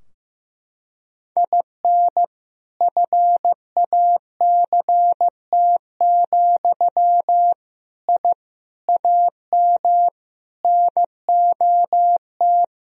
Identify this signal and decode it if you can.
Morse code, heard as a keyed tone, about 15 wpm, 700 Hz. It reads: IN FACT, I AM NOT
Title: Sentences from Top 1000 Words - Review Part 3 - 15wpm